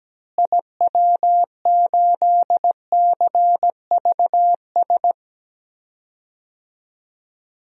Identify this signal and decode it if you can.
Morse code, heard as a keyed tone, about 17 wpm, 700 Hz. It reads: IW8CVS
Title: DXCC-100 Call Signs - 6 Characters - Repeated 2x - 17wpm